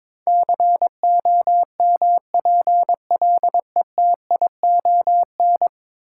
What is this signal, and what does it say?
Morse code, heard as a keyed tone, about 22 wpm, 700 Hz. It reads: COMPLETION
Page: morse-code-ninja-practice-22wpm